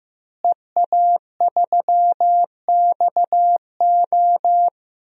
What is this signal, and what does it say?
Morse code, heard as a keyed tone, about 15 wpm, 700 Hz. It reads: EA3XO
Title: DXCC-100 Call Signs - 5 Characters - Repeated 3x - 15wpm